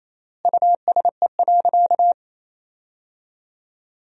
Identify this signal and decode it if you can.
Morse code, heard as a keyed tone, about 28 wpm, 700 Hz. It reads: USE.
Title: 3x Word-Spacing - Sentences from Top 100 Words - 28wpm